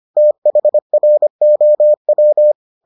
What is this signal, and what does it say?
Morse code, heard as a keyed tone, about 25 wpm, 600 Hz. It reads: THROW